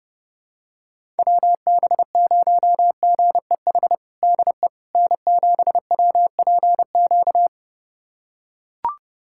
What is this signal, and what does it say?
Morse code, heard as a keyed tone, about 30 wpm, 700 Hz. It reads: WB0GEH DE N7WPQ
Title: USA Callsign DE Callsign 30wpm